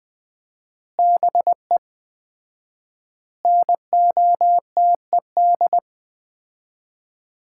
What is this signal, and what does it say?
Morse code, heard as a keyed tone, about 20 wpm, 700 Hz. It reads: BE NOTED
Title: 4x Word-Spacing - Sets of 5 Words - 20wpm